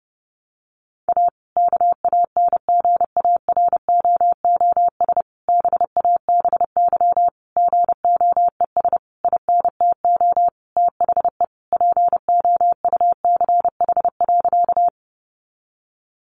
Morse code, heard as a keyed tone, about 30 wpm, 700 Hz. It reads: A KANGAROOS BABY GOES INTO THE POUCH.
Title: Easy Phrases 30wpm